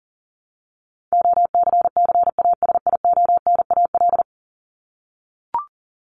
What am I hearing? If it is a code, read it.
Morse code, heard as a keyed tone, about 40 wpm, 700 Hz. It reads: OCCASIONAL
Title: LY Suffix - Top 100 Words with Base Words 40wpm